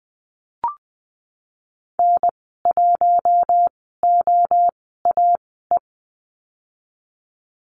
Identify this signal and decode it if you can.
Morse code, heard as a keyed tone, about 20 wpm, spaced as for 15 wpm, 700 Hz. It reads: N1OAE
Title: 147 Lesson 24g - Letter y - Call Signs 20wpm